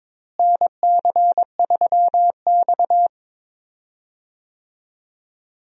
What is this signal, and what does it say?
Morse code, heard as a keyed tone, about 22 wpm, 700 Hz. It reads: NC3X
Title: USA Call Signs - 4 Characters - Repeated 2x - 22wpm